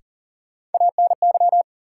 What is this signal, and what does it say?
Morse code, heard as a keyed tone, about 40 wpm, 700 Hz. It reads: ANY
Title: Sentences from Top 200 Words - New Word Drill - 40wpm